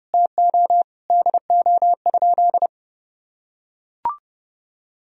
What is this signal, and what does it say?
Morse code, heard as a keyed tone, about 30 wpm, 700 Hz. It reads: TO DO?
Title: Sentences from Top 200 Words and Abbreviations 30wpm